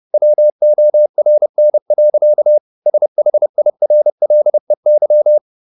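Morse code, heard as a keyed tone, about 30 wpm, 600 Hz. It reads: WORN. SHIRLEY
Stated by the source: Morse code at 30 wpm, the novel Never by Ken Follett in English, from CW med Gnister.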